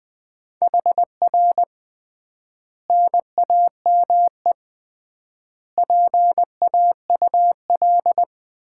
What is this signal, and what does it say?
Morse code, heard as a keyed tone, about 20 wpm, 700 Hz. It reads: HR NAME PAUL
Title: Common QSO Phrases - 3x Word-Spacing - 20wpm